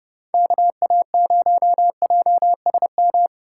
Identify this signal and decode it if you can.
Morse code, heard as a keyed tone, about 39 wpm, 700 Hz. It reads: KA0JSM